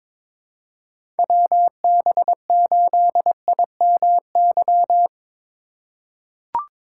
Morse code, heard as a keyed tone, about 22 wpm, 700 Hz. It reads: WB8IMY
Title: K1USN SST QSO Practice - 22wpm